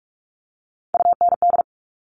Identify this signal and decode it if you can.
Morse code, heard as a keyed tone, about 46 wpm, 700 Hz. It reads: UND
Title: Top Three Letter Combinations in Words 45wpm